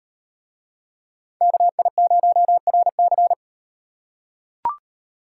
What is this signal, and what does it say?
Morse code, heard as a keyed tone, about 38 wpm, 700 Hz. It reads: KI0RC